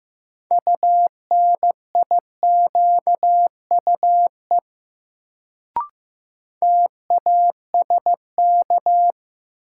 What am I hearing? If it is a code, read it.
Morse code, heard as a keyed tone, about 15 wpm, 700 Hz. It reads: UNIQUE TASK
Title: Sentences from Top 1000 Words - Review Part 7 - New Word Drill - 15wpm